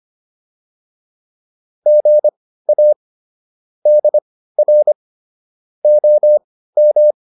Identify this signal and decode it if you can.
Morse code, heard as a keyed tone, about 25 wpm, spaced as for 15 wpm, 600 Hz. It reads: GA DR OM